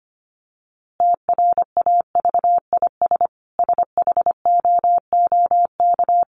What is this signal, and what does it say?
Morse code, heard as a keyed tone, about 25 wpm, 700 Hz. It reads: TRAVIS SHOOK
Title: A Sound Of Thunder 25wpm